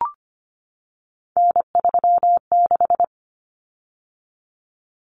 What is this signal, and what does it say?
Morse code, heard as a keyed tone, about 25 wpm, 700 Hz. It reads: N36
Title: Common Cut Numbers - 3-Digit 25wpm